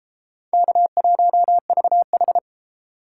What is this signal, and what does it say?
Morse code, heard as a keyed tone, about 31 wpm, 700 Hz. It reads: K1VH